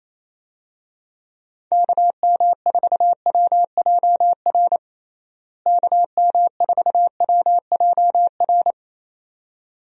Morse code, heard as a keyed tone, about 28 wpm, 700 Hz. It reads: KM4WJR KM4WJR
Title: USA Call Signs - 6 Characters - Repeated 2x - 28wpm